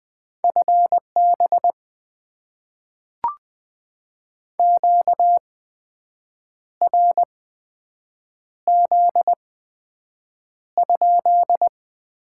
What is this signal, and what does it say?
Morse code, heard as a keyed tone, about 20 wpm, spaced as for 7 wpm, 700 Hz. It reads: FB QRZ?
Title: Top 90 QSO Elements - 8x Character Spacing - 20wpm